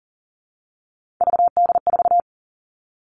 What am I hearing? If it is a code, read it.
Morse code, heard as a keyed tone, about 40 wpm, 700 Hz. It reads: VD4
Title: All Cut Numbers - 3-Digit 40wpm